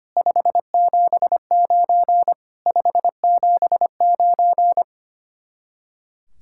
Morse code, heard as a keyed tone, about 31 wpm, 700 Hz. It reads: 579 579
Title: Speed-Racing Common QSO Phrases Encore - 25wpm